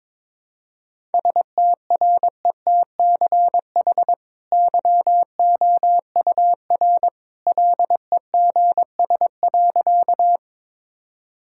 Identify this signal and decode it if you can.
Morse code, heard as a keyed tone, about 22 wpm, 700 Hz. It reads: STRETCH YOUR LEGS.
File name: practice / 22wpm / Easy Sentences with Hints - 22wpm